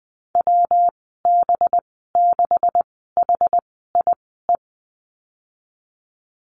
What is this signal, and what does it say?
Morse code, heard as a keyed tone, about 20 wpm, 700 Hz. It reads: W B 6 H I E
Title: 207 Lesson 33g - Letter j - Call Signs 20wpm